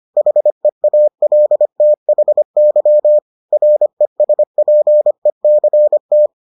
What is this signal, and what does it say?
Morse code, heard as a keyed tone, about 25 wpm, 600 Hz. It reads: HEALTHY RESPECT